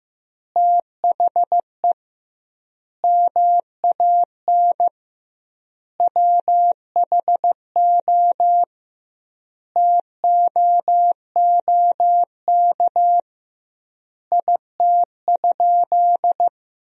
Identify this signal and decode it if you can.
Morse code, heard as a keyed tone, about 16 wpm, 700 Hz. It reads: THE MAN WHO TOOK IT?
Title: 2x Word-Spacing - Sentences from Top 300 Words - 15wpm